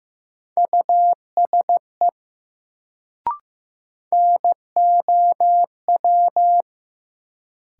Morse code, heard as a keyed tone, about 15 wpm, 700 Hz. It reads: USE NOW
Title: Sentences from Top 100 Words - New Word Drill - 15wpm